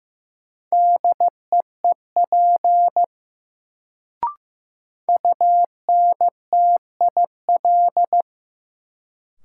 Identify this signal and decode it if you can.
Morse code, heard as a keyed tone, about 15 wpm, 700 Hz. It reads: DEEP UNTIL
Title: Sets of 5 Words Encore - New Word Drill - 15wpm